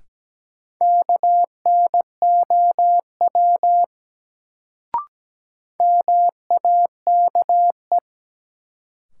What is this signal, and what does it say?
Morse code, heard as a keyed tone, about 17 wpm, 700 Hz. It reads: KNOW MAKE
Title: Sentences from Top 100 Words - New Word Drill - 17wpm